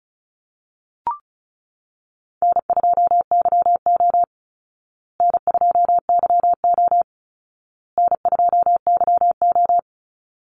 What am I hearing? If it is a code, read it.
Morse code, heard as a keyed tone, about 34 wpm, 700 Hz. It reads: N2YO N2YO N2YO
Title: Easy Call Signs Repeated 3x - 35wpm